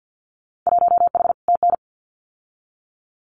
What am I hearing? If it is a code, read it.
Morse code, heard as a keyed tone, about 50 wpm, 700 Hz. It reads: 1H TN
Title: Winter Field Day Exchanges 50wpm